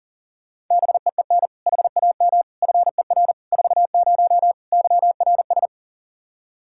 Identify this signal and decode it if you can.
Morse code, heard as a keyed tone, about 40 wpm, 700 Hz. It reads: BEEN HAM FER 40 YRS
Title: Practice QSO - Exchange 3 - More Info - 40wpm